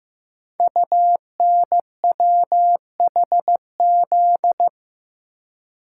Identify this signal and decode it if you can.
Morse code, heard as a keyed tone, about 15 wpm, 700 Hz. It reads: UNWHZ
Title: Code Groups - 5 Letters - 15wpm